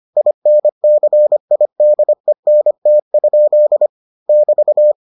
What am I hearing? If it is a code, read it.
Morse code, heard as a keyed tone, about 25 wpm, 600 Hz. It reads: INCIDENT? <BT>